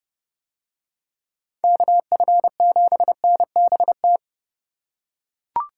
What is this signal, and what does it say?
Morse code, heard as a keyed tone, about 29 wpm, 700 Hz. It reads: KF7NBT